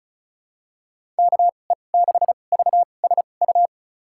Morse code, heard as a keyed tone, about 36 wpm, spaced as for 25 wpm, 700 Hz. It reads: KE6VSU
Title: USA Call Signs - 6 Characters - 2x Character Spacing - 35wpm